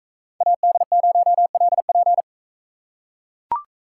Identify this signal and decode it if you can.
Morse code, heard as a keyed tone, about 42 wpm, 700 Hz. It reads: AD0LP